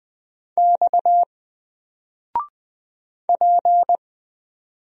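Morse code, heard as a keyed tone, about 20 wpm, 700 Hz. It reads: X P
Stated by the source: Single letter and number Morse drill at 20 wpm, from N4LCD.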